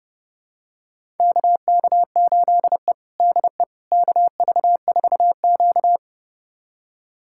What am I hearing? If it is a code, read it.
Morse code, heard as a keyed tone, about 30 wpm, 700 Hz. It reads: KK8E DE KV4Q
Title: USA Callsign DE Callsign - 4 Characters - 30wpm